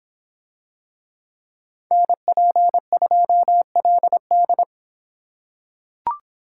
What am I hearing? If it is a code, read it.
Morse code, heard as a keyed tone, about 24 wpm, 700 Hz. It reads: NP2LD